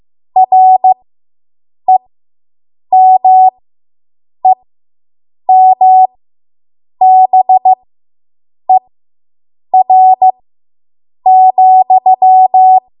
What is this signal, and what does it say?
Morse code, heard as a keyed tone, about 15 wpm, spaced as for 7 wpm, 750 Hz. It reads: REMEMBER,